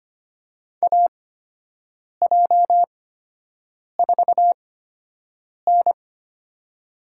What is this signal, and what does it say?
Morse code, heard as a keyed tone, about 25 wpm, spaced as for 7 wpm, 700 Hz. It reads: AJ4N
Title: USA Call Signs - 4 Characters - 8x Character Spacing - 25wpm